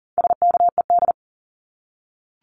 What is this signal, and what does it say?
Morse code, heard as a keyed tone, about 39 wpm, 700 Hz. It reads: SKED
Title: Abbreviations Repeated 3x - 40wpm